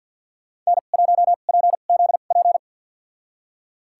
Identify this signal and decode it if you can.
Morse code, heard as a keyed tone, about 47 wpm, spaced as for 35 wpm, 700 Hz. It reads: N1PZP
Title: USA Call Signs - 5 Characters - 2x Character Spacing - 50wpm